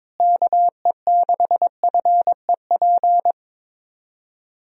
Morse code, heard as a keyed tone, about 22 wpm, 700 Hz. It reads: KE6FEP
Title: USA Call Signs - 6 Characters - Repeated 2x - 22wpm